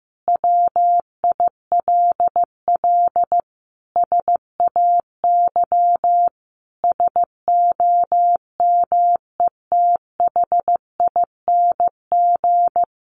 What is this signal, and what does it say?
Morse code, heard as a keyed tone, about 15 wpm, 700 Hz. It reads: WILL SAY SOMETHING
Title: Sentences from Top 1500 Words 15wpm